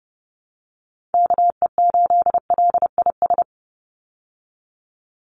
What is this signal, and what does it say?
Morse code, heard as a keyed tone, about 30 wpm, 700 Hz. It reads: KE8LIS
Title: USA Call Signs 30wpm